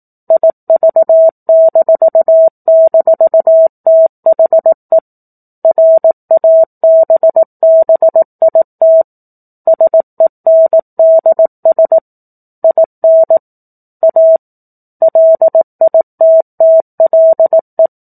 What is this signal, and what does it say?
Morse code, heard as a keyed tone, about 18 wpm, 650 Hz. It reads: IV--THE RABBIT SENDS IN A LITTLE
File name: 04-Chapter-04 18 WPM